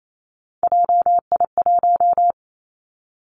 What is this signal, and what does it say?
Morse code, heard as a keyed tone, about 28 wpm, 700 Hz. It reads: JI1
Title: Three Challenging Letter-Number Combinations 28wpm